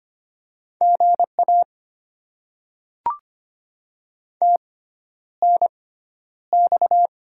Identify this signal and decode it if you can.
Morse code, heard as a keyed tone, about 25 wpm, spaced as for 11 wpm, 700 Hz. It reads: GA TNX
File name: Top 20 QSO Elements - 6x Character Spacing - 25wpm